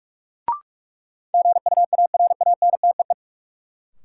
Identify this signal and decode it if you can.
Morse code, heard as a keyed tone, about 41 wpm, 700 Hz. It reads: GUARANTEE
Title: Sentences from Top 1600 Words - New Word Drill Alt - 45wpm